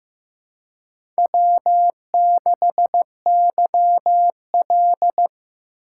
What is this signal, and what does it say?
Morse code, heard as a keyed tone, about 15 wpm, 700 Hz. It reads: W6YL
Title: USA Call Signs - 4 Characters - 2x Character Spacing - 15wpm